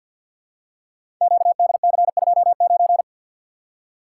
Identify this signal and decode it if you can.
Morse code, heard as a keyed tone, about 51 wpm, 690 Hz. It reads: QDC29
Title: Code Groups - 5 Letter:Numbers - 50wpm